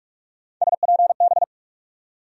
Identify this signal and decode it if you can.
Morse code, heard as a keyed tone, about 44 wpm, 700 Hz. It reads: SPB
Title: USA Call Sign Postfixes - 3 Letters 45wpm